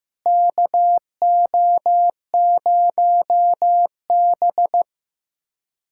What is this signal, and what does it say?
Morse code, heard as a keyed tone, about 16 wpm, 700 Hz. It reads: KO0B
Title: USA Callsign DE Callsign - 4 Characters - 15wpm